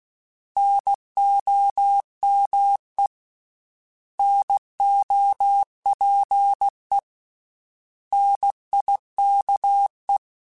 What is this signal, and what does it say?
Morse code, heard as a keyed tone, about 16 wpm, 800 Hz. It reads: NOME NOPE NIKE
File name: Misc / Misc-mp3s / Misc @15